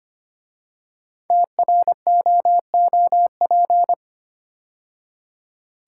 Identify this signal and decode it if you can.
Morse code, heard as a keyed tone, about 25 wpm, 700 Hz. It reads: TROOP
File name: 5 Letter Words - First 3 Letters Given - 25wpm